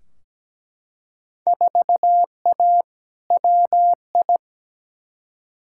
Morse code, heard as a keyed tone, about 17 wpm, 700 Hz. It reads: 4A WI
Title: Field Day Exchanges - 17wpm